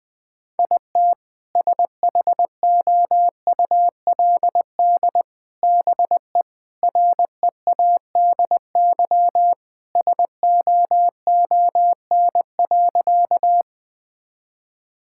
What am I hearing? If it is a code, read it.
Morse code, heard as a keyed tone, about 21 wpm, 700 Hz. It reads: IT SHOULD BE READY SOON.
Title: Sentences with 2-Word Phrases - 20wpm